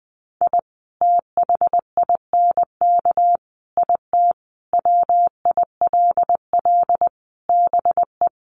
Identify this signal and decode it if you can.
Morse code, heard as a keyed tone, about 20 wpm, 700 Hz. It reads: I THINK IT WILL BE
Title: Sets of 5 Words Encore2 20wpm